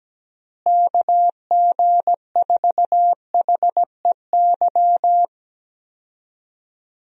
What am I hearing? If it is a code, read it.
Morse code, heard as a keyed tone, about 17 wpm, 700 Hz. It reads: KG4HEY
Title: USA Call Signs - 6 Characters - Repeated 3x - 17wpm